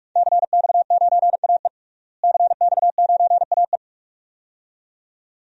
Morse code, heard as a keyed tone, about 44 wpm, 700 Hz. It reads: CX9AE CX9AE
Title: DXCC-100 Call Signs - 5 Characters - Repeated 2x - 45wpm